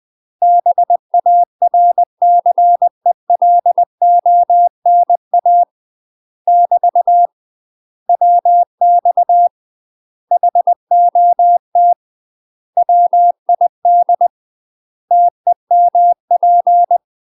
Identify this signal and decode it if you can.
Morse code, heard as a keyed tone, about 20 wpm, 700 Hz. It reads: BARCELONA <BT> WX HOT WID TEMP
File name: UK MASTER QSOs TRAINING/20WPM